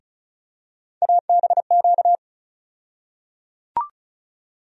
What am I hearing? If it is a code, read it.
Morse code, heard as a keyed tone, about 36 wpm, 700 Hz. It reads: ABQ
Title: USA Call Sign Postfixes - 3 Letters 35wpm